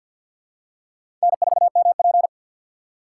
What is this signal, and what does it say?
Morse code, heard as a keyed tone, about 48 wpm, 700 Hz. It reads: N4MP
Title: USA Call Signs - 4 Characters - 2x Character Spacing - 50wpm